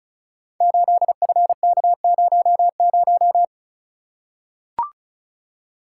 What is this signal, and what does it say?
Morse code, heard as a keyed tone, about 36 wpm, 700 Hz. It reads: 8FK00
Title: Code Groups - 5 Letter:Numbers - 35wpm